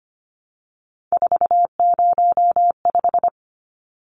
Morse code, heard as a keyed tone, about 25 wpm, 700 Hz. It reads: 405
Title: Numbers - 3-Digit 25wpm